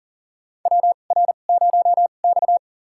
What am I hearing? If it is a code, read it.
Morse code, heard as a keyed tone, about 40 wpm, spaced as for 29 wpm, 700 Hz. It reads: WR0X
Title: USA Call Signs - 4 Characters - 2x Character Spacing - 40wpm